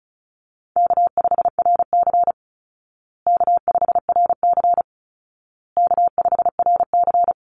Morse code, heard as a keyed tone, about 34 wpm, 700 Hz. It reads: K5RC K5RC K5RC
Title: Call Signs Repeated 2x - 35wpm